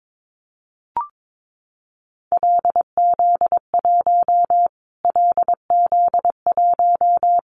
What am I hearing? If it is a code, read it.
Morse code, heard as a keyed tone, about 22 wpm, 700 Hz. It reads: LZ1 LZ1
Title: Callsign Prefixes for Top 25 DXCC - Repeated 3x - 22wpm